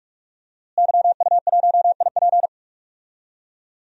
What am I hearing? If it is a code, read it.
Morse code, heard as a keyed tone, about 41 wpm, 700 Hz. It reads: YU1IP
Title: DXCC-100 Call Signs - 5 Characters - Repeated 3x - 45wpm